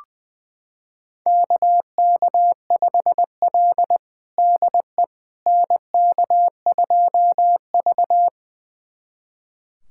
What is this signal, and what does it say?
Morse code, heard as a keyed tone, about 21 wpm, 700 Hz. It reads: KK5L DE NK2V
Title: USA Callsign DE Callsign - 4 Characters - 20wpm